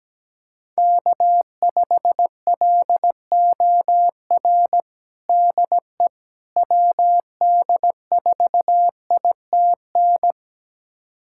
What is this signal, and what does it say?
Morse code, heard as a keyed tone, about 17 wpm, 700 Hz. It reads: K5LOR DE WD4ITN
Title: USA Callsign DE Callsign 17wpm